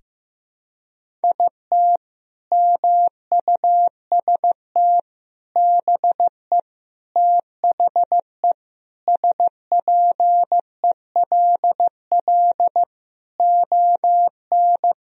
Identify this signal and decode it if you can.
Morse code, heard as a keyed tone, about 16 wpm, 700 Hz. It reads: IT MUST BE THE SPELL ON
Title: Sentences from Top 1000 Words - Review Part 1 - 15wpm